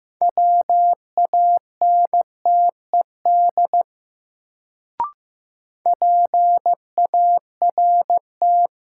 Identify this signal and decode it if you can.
Morse code, heard as a keyed tone, about 15 wpm, 700 Hz. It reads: WANTED PART
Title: Sets of 2 Words - New Word Drill - 15wpm